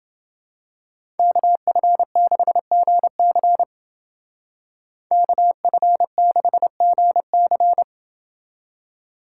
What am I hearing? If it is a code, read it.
Morse code, heard as a keyed tone, about 28 wpm, 700 Hz. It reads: KF6GC KF6GC